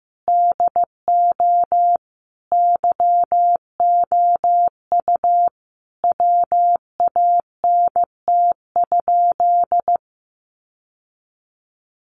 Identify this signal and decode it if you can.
Morse code, heard as a keyed tone, about 15 wpm, 700 Hz. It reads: DO YOU WANT?
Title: Sentences from Top 1100 Words 15wpm